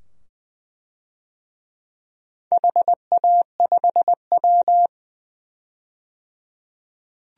Morse code, heard as a keyed tone, about 20 wpm, 700 Hz. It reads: HA5W
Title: DXCC-100 Call Signs - 4 Characters - 20wpm